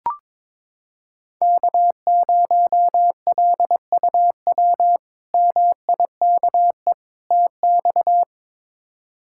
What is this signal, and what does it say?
Morse code, heard as a keyed tone, about 22 wpm, 700 Hz. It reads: K0LUW MIKE TX
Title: K1USN SST QSO Practice - 22wpm